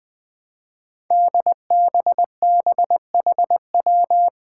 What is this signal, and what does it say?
Morse code, heard as a keyed tone, about 20 wpm, 700 Hz. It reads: DBBHW
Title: Five Challenging Letter-Number Combinations 20wpm